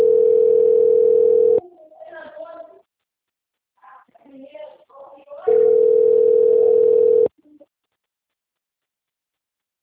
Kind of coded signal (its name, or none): telephone call progress tone